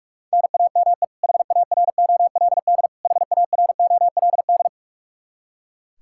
Morse code, heard as a keyed tone, about 42 wpm, 700 Hz. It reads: NAME HAROLD HAROLD
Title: K1USN SST QSO Practice 45wpm